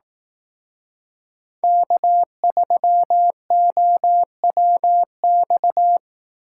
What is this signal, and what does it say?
Morse code, heard as a keyed tone, about 18 wpm, 700 Hz. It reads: K3OWX
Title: Speed Racing - USA 1x3 Call Signs 15wpm